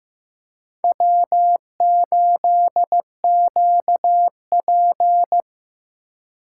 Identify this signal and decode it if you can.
Morse code, heard as a keyed tone, about 16 wpm, 700 Hz. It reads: W8QP
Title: USA Call Signs - 4 Characters - 3x Character Spacing - 15wpm